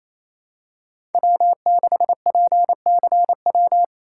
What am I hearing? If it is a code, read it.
Morse code, heard as a keyed tone, about 27 wpm, 700 Hz. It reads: W6PCW
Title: USA Call Signs - 4x Character Spacing - 28wpm